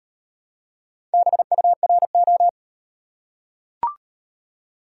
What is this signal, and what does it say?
Morse code, heard as a keyed tone, about 34 wpm, 700 Hz. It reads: BURO